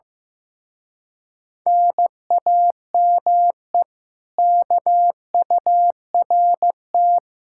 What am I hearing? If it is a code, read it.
Morse code, heard as a keyed tone, about 15 wpm, 700 Hz. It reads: NAME KURT
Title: Speed-Racing Common QSO Phrases Encore - 15wpm